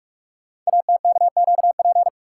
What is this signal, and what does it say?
Morse code, heard as a keyed tone, about 44 wpm, 700 Hz. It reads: ATKQP
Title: Code Groups - 5 Letters - 45wpm